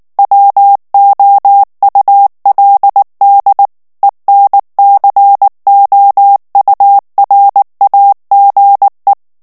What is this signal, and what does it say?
Morse code, heard as a keyed tone, about 19 wpm, 800 Hz. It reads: WOULD ENCOURAGE
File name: ANGLAIS texte clair 3 vitesse 18 mots minute